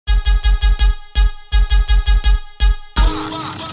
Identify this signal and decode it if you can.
There is no code.